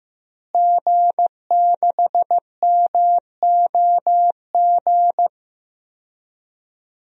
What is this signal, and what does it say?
Morse code, heard as a keyed tone, about 15 wpm, 700 Hz. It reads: G6MOG
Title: DXCC-100 Call Signs - 5 Characters - Repeated 3x - 15wpm